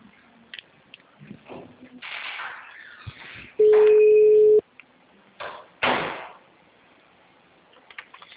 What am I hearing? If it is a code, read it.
Ringback tone (Europe, 425 Hz).